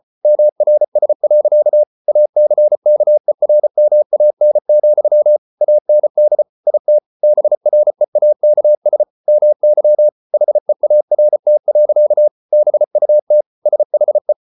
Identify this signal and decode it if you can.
Morse code, heard as a keyed tone, about 34 wpm, 600 Hz. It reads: MRS. ACKERMAN, AND IT BREAKS MY HEART. BUT SHE